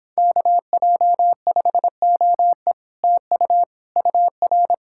Morse code, heard as a keyed tone, about 31 wpm, 700 Hz. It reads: KJ5OE TU UR